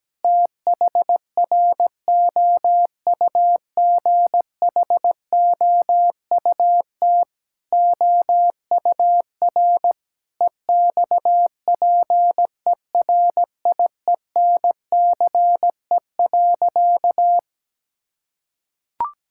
Morse code, heard as a keyed tone, about 17 wpm, 700 Hz. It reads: THROUGHOUT OUR EXPERIENCE.
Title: Sentences from Top 1000 Words - Review Part 5 - 17wpm